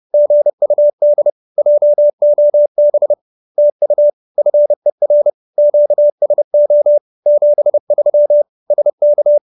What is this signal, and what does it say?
Morse code, heard as a keyed tone, about 30 wpm, 590 Hz. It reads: GUD JOB TU FER QSO 73 SK
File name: QSO308 30